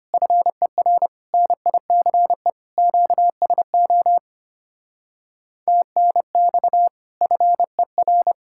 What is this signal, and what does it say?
Morse code, heard as a keyed tone, about 30 wpm, 700 Hz. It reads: FER NICE QSO TNX FER
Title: Speed-Racing Common QSO Phrases - 25wpm